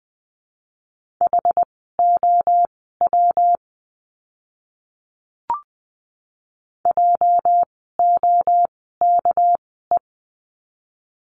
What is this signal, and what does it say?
Morse code, heard as a keyed tone, about 20 wpm, spaced as for 15 wpm, 700 Hz. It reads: HOW JOKE
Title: 206 Lesson 33f - Letter j - Words 20wpm